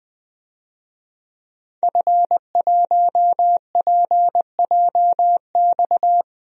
Morse code, heard as a keyed tone, about 20 wpm, 700 Hz. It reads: F1PJX
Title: DXCC-100 Call Signs - 5 Characters - Repeated 3x - 20wpm